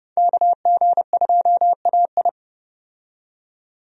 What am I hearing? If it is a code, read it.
Morse code, heard as a keyed tone, about 30 wpm, 700 Hz. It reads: KG2AI
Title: USA Call Signs - 5 Characters - 30wpm